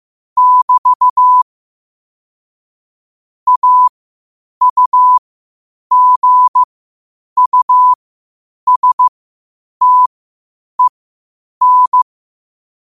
Morse code, heard as a keyed tone, about 15 wpm, spaced as for 8 wpm, 1000 Hz. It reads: <BT> AUGUSTEN
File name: qotdfiles / 8wpm